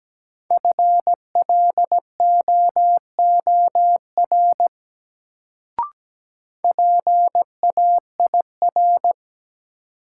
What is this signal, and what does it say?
Morse code, heard as a keyed tone, about 17 wpm, 700 Hz. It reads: FLOOR PAIR
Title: Sets of 3 Words - New Word Drill - 17wpm